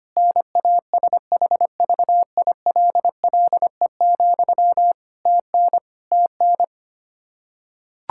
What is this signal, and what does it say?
Morse code, heard as a keyed tone, about 25 wpm, 700 Hz. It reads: NASHVILLE, TN TN
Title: Practice QSO - Exchange 1 - Beginning - 25wpm